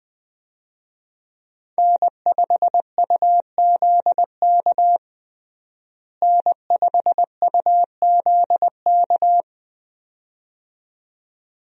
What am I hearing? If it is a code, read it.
Morse code, heard as a keyed tone, about 20 wpm, 700 Hz. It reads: N5UZK N5UZK
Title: USA Call Signs - 5 Characters - Repeated 2x - 20wpm